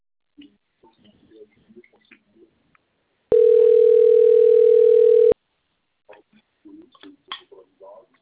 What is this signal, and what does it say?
Ringback tone (North America), 440 + 480 Hz.